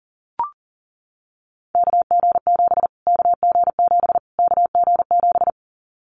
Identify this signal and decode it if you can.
Morse code, heard as a keyed tone, about 40 wpm, 700 Hz. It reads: KG7 KG7 KG7
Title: Callsign Prefixes for Top 25 DXCC - Repeated 3x - 40wpm